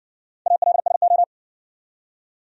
Morse code, heard as a keyed tone, about 57 wpm, 700 Hz. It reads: ALSO